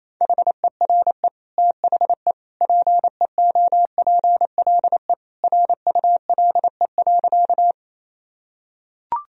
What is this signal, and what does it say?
Morse code, heard as a keyed tone, about 28 wpm, 700 Hz. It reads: HERE THE PEOPLE RULE.
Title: Sentences from Top 1000 Words - Review Part 2 - 28wpm